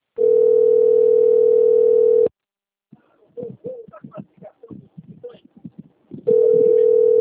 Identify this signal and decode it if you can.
Ringback tone (North America), 440 + 480 Hz, with speech.
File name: rg-1600-unknown- - - .12821